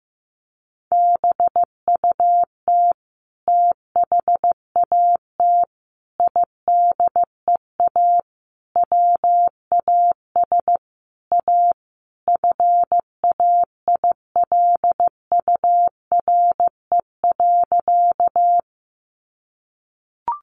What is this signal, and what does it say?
Morse code, heard as a keyed tone, about 15 wpm, 700 Hz. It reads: BUT THAT IDEA WAS A FAILURE.
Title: Sentences from Top 900 Words 15wpm